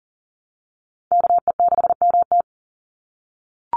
Morse code, heard as a keyed tone, about 40 wpm, 700 Hz. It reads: KE6MT
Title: NA SOTA Chaser Callsigns 40wpm